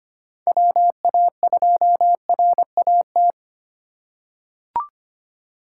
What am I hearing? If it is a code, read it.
Morse code, heard as a keyed tone, about 25 wpm, 700 Hz. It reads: WA2RAT
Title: USA Call Signs - 6 Characters - 2x Character Spacing - 25wpm